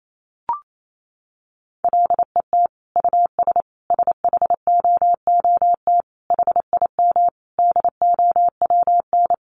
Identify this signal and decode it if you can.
Morse code, heard as a keyed tone, about 28 wpm, 700 Hz. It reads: LET US SHOOT HIM DOWN
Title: Sentences from Top 1200 Words 28wpm